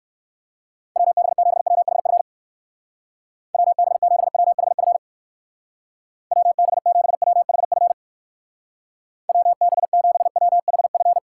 Morse code, heard as a keyed tone, about 56 wpm, 700 Hz. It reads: WB7WHF WB7WHF WB7WHF WB7WHF